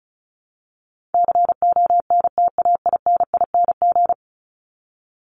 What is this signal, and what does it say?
Morse code, heard as a keyed tone, about 34 wpm, 700 Hz. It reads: CONTAINING
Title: ING Suffix - Top 100 Words with Base Words 35wpm